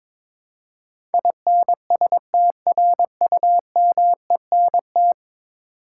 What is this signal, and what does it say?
Morse code, heard as a keyed tone, about 22 wpm, 700 Hz. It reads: INSTRUMENT
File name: practice / 22wpm / 10 Letter Words - First 3 Letters Given - 22wpm